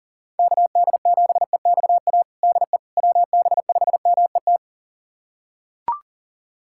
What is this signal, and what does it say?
Morse code, heard as a keyed tone, about 40 wpm, 700 Hz. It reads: KD7EXA DE WB5MET
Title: USA Callsign DE Callsign - 6 Characters - 40wpm